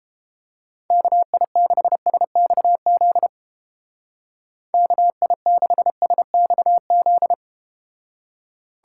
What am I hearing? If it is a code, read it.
Morse code, heard as a keyed tone, about 30 wpm, 700 Hz. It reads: KI6SXZ KI6SXZ